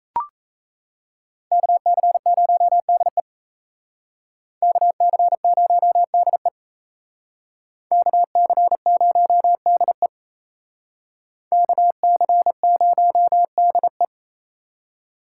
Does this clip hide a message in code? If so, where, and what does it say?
Yes: Morse code from 1.5 to 14.1 s, heard as a keyed tone, about 36 wpm, 700 Hz. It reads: KC0DE KC0DE KC0DE KC0DE